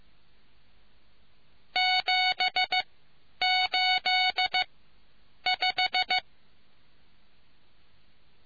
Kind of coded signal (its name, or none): Morse code